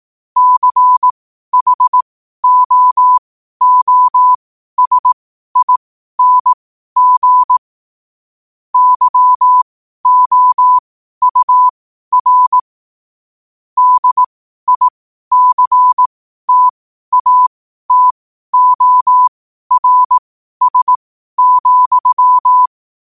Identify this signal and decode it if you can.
Morse code, heard as a keyed tone, about 18 wpm, spaced as for 12 wpm, 1000 Hz. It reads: CHOOSING YOUR DICTATORS,
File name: qotdfiles / 12wpm